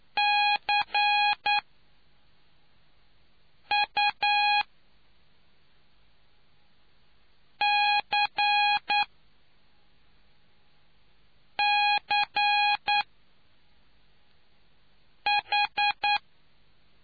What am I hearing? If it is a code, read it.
Morse code, heard as a keyed tone, about 9 wpm, spaced as for 3 wpm, 830 Hz. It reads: CUCCH